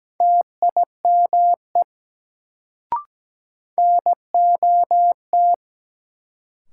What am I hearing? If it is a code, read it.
Morse code, heard as a keyed tone, about 17 wpm, 700 Hz. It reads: TIME NOT
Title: Sets of 2 Words - New Word Drill - 17wpm